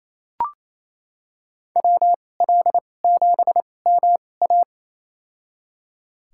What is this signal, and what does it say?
Morse code, heard as a keyed tone, about 28 wpm, spaced as for 20 wpm, 700 Hz. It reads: WL7MA
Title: USA Call Signs - 5 Characters - 2x Character Spacing - 28wpm